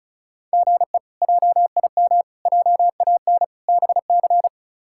Morse code, heard as a keyed tone, about 34 wpm, 700 Hz. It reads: GE JIM JAN BC
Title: K1USN SST QSO Practice - 35wpm